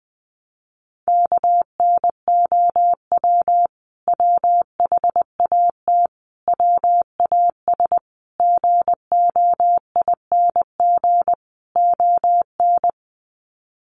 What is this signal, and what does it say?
Morse code, heard as a keyed tone, about 20 wpm, 700 Hz. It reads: KNOW WHAT WAS GOING ON
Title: Sets of 5 Words Encore2 20wpm